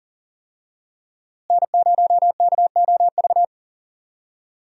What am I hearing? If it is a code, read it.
Morse code, heard as a keyed tone, about 40 wpm, 700 Hz. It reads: N0KOV
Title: USA Call Signs - 5 Characters - Repeated 3x - 40wpm